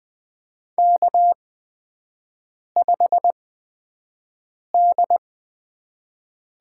Morse code, heard as a keyed tone, about 21 wpm, spaced as for 5 wpm, 700 Hz. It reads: K5D
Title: USA Call Signs - 4 Characters - 8x Character Spacing - 20wpm